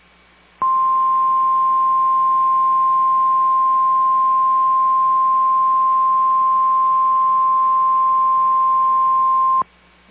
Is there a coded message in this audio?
Yes: Emergency Alert System (SAME) tones (the 1050 Hz attention tone) from 0.6 to 9.7 s.